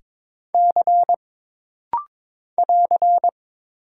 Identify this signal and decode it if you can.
Morse code, heard as a keyed tone, about 22 wpm, 700 Hz. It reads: C <AR>
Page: This rapid-fire timing is geared to push your ICR (Instant Character Recognition) skills.